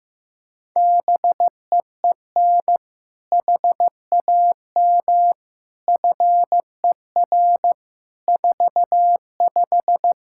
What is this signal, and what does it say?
Morse code, heard as a keyed tone, about 15 wpm, 700 Hz. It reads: BEEN HAM FER 45
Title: Practice QSO - Exchange 3 - More Info - 15wpm